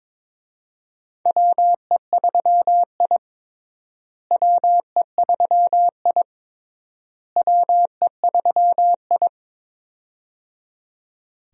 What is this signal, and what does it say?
Morse code, heard as a keyed tone, about 22 wpm, 700 Hz. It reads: WE3I WE3I WE3I
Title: USA Call Signs - 4 Characters - Repeated 3x - 22wpm